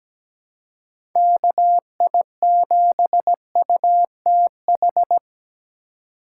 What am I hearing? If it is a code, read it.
Morse code, heard as a keyed tone, about 17 wpm, 700 Hz. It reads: KI7UTH
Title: USA Call Signs - 6 Characters - 2x Character Spacing - 17wpm